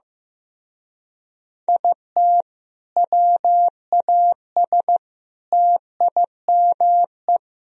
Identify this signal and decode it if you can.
Morse code, heard as a keyed tone, about 16 wpm, 700 Hz. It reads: IT WAS TIME
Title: Speed-Racing - Sets of 4 Words 15wpm